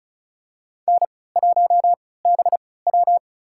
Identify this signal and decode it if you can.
Morse code, heard as a keyed tone, about 36 wpm, spaced as for 20 wpm, 700 Hz. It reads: N1BW
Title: USA Call Signs - 4 Characters - 3x Character Spacing - 35wpm